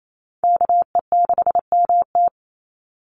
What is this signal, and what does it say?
Morse code, heard as a keyed tone, about 28 wpm, 700 Hz. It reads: KE6MT
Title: NA SOTA Chaser Callsigns 28wpm